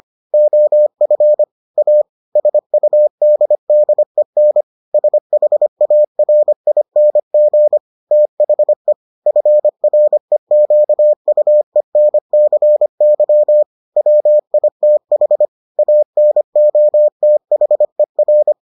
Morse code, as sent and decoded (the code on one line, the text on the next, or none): --- ..-. / .- / ... ..- -.. -.. -. / ... .... .- .-. .. -. --. / - .... / ..-. .-. --.- ..- . -. -.-. -.-- / .-- .. - .... / .- -. --- - .... .-.
OF A SUDDEN SHARING THE FREQUENCY WITH ANOTHER